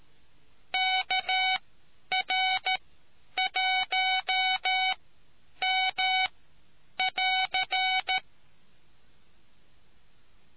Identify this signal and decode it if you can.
Morse code, heard as a keyed tone, about 13 wpm, spaced as for 9 wpm, 750 Hz. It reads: KR1M <AR>